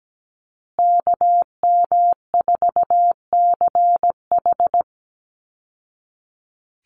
Morse code, heard as a keyed tone, about 17 wpm, 700 Hz. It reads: KM4CH
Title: Call Signs Repeated 2x - 17wpm